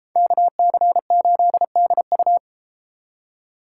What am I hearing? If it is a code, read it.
Morse code, heard as a keyed tone, about 33 wpm, 700 Hz. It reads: KC8DU